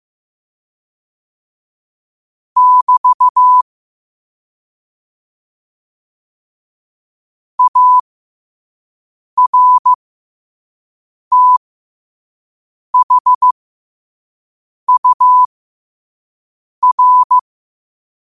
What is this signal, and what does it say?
Morse code, heard as a keyed tone, about 15 wpm, spaced as for 5 wpm, 1000 Hz. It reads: <BT> ARTHUR
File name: qotdfiles / 5wpm